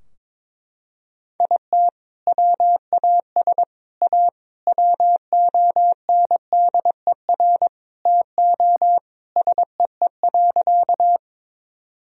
Morse code, heard as a keyed tone, about 22 wpm, 700 Hz. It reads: IT WAS A WONDER TO SEE.
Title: Sentences from Top 1000 Words - Review Part 3 - 22wpm